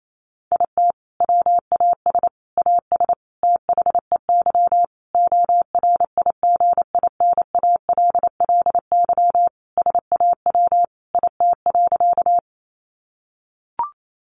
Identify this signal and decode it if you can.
Morse code, heard as a keyed tone, about 28 wpm, 700 Hz. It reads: IT WAS AS THEY ORIGINALLY SAW IT.
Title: Sentences from Top 1500 Words 28wpm